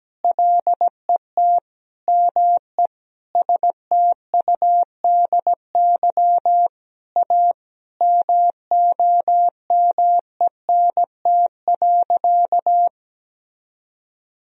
Morse code, heard as a keyed tone, about 17 wpm, 700 Hz. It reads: LET ME STUDY A MOMENT.
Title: Sentences from Top 1000 Words - Review Part 1 - 17wpm